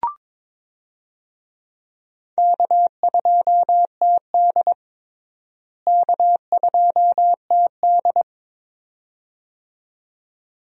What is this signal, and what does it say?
Morse code, heard as a keyed tone, about 22 wpm, 700 Hz. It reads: K2TD K2TD
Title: USA Call Signs - 4 Characters - Repeated 2x - 22wpm